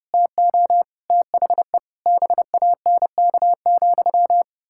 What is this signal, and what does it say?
Morse code, heard as a keyed tone, about 30 wpm, 700 Hz. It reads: TO THE BANK,
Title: Pangram Challenge v2 - 30wpm